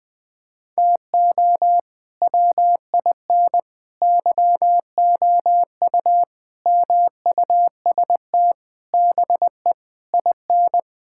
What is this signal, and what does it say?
Morse code, heard as a keyed tone, about 20 wpm, 700 Hz. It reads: TO WIN YOU MUST BE IN
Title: Sentences with 9-Word Phrases - 20wpm